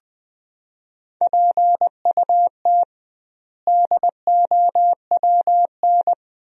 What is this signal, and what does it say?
Morse code, heard as a keyed tone, about 20 wpm, 700 Hz. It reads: PUT DOWN
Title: 2x Word-Spacing - Sentences from Top 400 Words - 20wpm